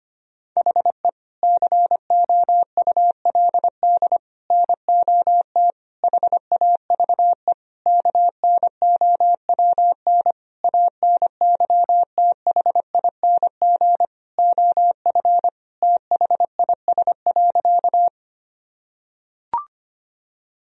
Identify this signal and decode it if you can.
Morse code, heard as a keyed tone, about 25 wpm, 700 Hz. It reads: HE COULD NOT HAVE KNOWN ANYTHING OF THIS.
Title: Sentences from Top 1000 Words - Review Part 3 - 25wpm